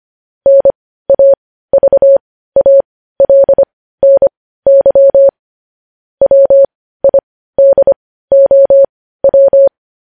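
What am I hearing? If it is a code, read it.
Morse code, heard as a keyed tone, about 25 wpm, spaced as for 15 wpm, 550 Hz. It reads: NAVALNY WIDOW
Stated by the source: Morse code news headlines sent at 15 words per minute.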